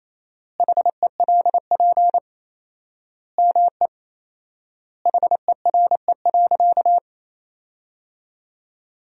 Morse code, heard as a keyed tone, about 28 wpm, 700 Hz. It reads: HELP ME HERE.
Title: 4x Word-Spacing - Sentences from Top 200 Words - 28wpm